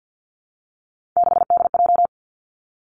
Morse code, heard as a keyed tone, about 50 wpm, 700 Hz. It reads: 6DJ
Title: Three Challenging Letter-Number Combinations 50wpm